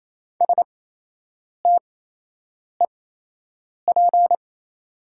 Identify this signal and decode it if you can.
Morse code, heard as a keyed tone, about 28 wpm, spaced as for 8 wpm, 700 Hz. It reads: STEP
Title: 8x Character Spacing - Top 500 Words - 28wpm